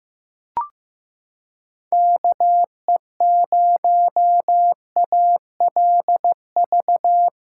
Morse code, heard as a keyed tone, about 15 wpm, 700 Hz. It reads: KE0ALV
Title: USA Callsign DE Callsign - 6 Characters - 15wpm